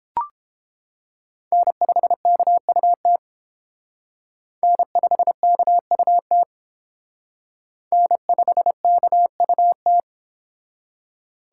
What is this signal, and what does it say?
Morse code, heard as a keyed tone, about 31 wpm, 700 Hz. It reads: N5KUT N5KUT N5KUT